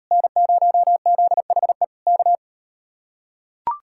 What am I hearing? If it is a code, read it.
Morse code, heard as a keyed tone, about 38 wpm, 700 Hz. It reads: N0ZHE K